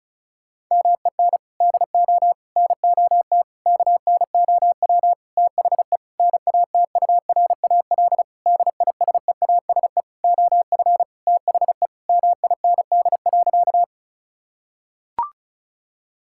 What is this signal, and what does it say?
Morse code, heard as a keyed tone, about 36 wpm, 700 Hz. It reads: MEN DO NOT KNOW THE NATURAL DISEASE OF THE MIND.
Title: Sentences from Top 1000 Words - Review Part 6 - 35wpm